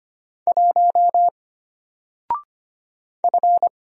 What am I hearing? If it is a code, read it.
Morse code, heard as a keyed tone, about 25 wpm, 700 Hz. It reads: 1 F
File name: Single-Letter-Number-25wpm